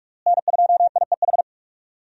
Morse code, heard as a keyed tone, about 41 wpm, 700 Hz. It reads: N2IEH
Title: USA Call Signs - 5 Characters - Repeated 3x - 45wpm